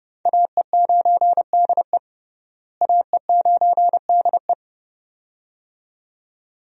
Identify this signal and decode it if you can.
Morse code, heard as a keyed tone, about 29 wpm, 700 Hz. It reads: AE9DE AE9DE
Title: USA Call Signs - 5 Characters - Repeated 2x - 30wpm